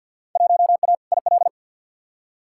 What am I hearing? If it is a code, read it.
Morse code, heard as a keyed tone, about 46 wpm, 700 Hz. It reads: 1A IL
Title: Field Day Exchanges - 50wpm